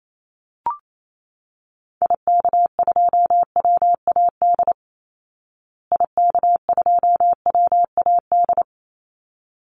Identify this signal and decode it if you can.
Morse code, heard as a keyed tone, about 28 wpm, 700 Hz. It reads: IK2WAD IK2WAD
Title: Call Signs Repeated 3x - 28wpm